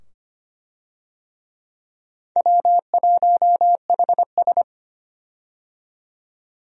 Morse code, heard as a keyed tone, about 25 wpm, 700 Hz. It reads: W1HS
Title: USA Call Signs - 4 Characters - 25wpm